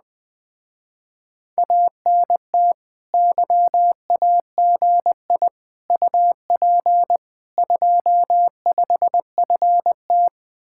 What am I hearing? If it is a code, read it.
Morse code, heard as a keyed tone, about 20 wpm, 700 Hz. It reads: ANT YAGI UP 25FT
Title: Speed-Racing Common QSO Phrases - 17wpm